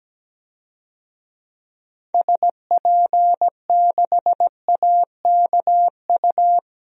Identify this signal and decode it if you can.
Morse code, heard as a keyed tone, about 17 wpm, 700 Hz. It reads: SP6AKU
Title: DXCC-100 Call Signs - 6 Characters - 17wpm